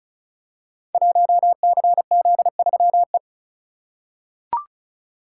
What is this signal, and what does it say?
Morse code, heard as a keyed tone, about 36 wpm, 690 Hz. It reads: 1CZ3E
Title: Code Groups - 5 Letter:Numbers - 35wpm